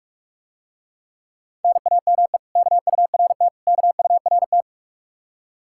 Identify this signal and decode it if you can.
Morse code, heard as a keyed tone, about 44 wpm, 700 Hz. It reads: NAME KURT KURT